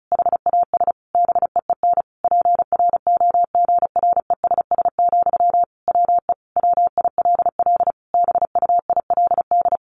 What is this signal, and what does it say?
Morse code, heard as a keyed tone, about 34 wpm, 700 Hz. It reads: HAS BEEN PROGRESS, WE WILL BUILD
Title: Sentences from Top 1400 Words 35wpm